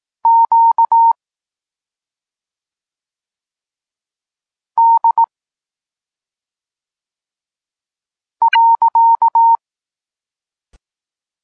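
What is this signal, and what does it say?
Morse code, heard as a keyed tone, about 18 wpm, 920 Hz. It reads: Q D .
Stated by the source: Polish Morse lesson teaching the letter C, from SP9MOA Niepołomicki Klub Krótkofalowców.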